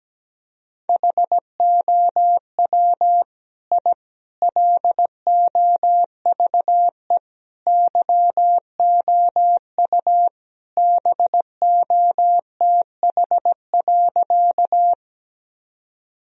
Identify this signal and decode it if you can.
Morse code, heard as a keyed tone, about 17 wpm, 700 Hz. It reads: HOW I LOVE YOU BOTH.
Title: Sentences from Top 1000 Words - Review Part 1 - 17wpm